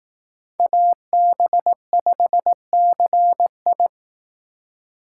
Morse code, heard as a keyed tone, about 18 wpm, 700 Hz. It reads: AB5CI
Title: Speed Racing - USA 2x2 Call Signs 15wpm